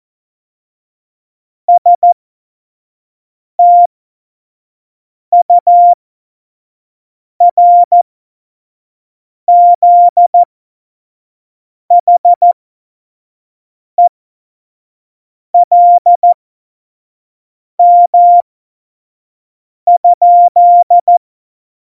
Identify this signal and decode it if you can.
Morse code, heard as a keyed tone, about 14 wpm, spaced as for 5 wpm, 700 Hz. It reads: STURZHELM?
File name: Morsekurs / Witze / Witz 20. Oktober 2020, 25 F70